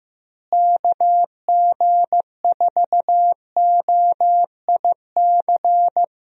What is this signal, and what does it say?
Morse code, heard as a keyed tone, about 15 wpm, 700 Hz. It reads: KG4OIC
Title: USA Call Signs - 6 Characters - Repeated 2x - 15wpm